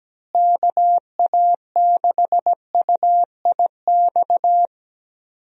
Morse code, heard as a keyed tone, about 17 wpm, 700 Hz. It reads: KA6UIX
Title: USA Call Signs - 6 Characters - Repeated 2x - 17wpm